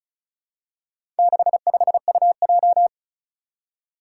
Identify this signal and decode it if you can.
Morse code, heard as a keyed tone, about 34 wpm, 700 Hz. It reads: 65UJ
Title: Four Challenging Letter-Number Combinations 35wpm